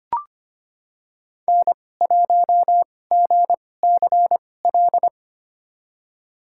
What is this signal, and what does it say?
Morse code, heard as a keyed tone, about 25 wpm, spaced as for 18 wpm, 700 Hz. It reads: N1GCL
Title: USA Call Signs - 2x Character Spacing - 25wpm